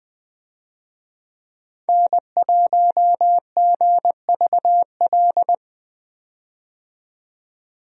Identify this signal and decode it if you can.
Morse code, heard as a keyed tone, about 20 wpm, 700 Hz. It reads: N1GVL
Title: USA Call Signs - 5 Characters - 20wpm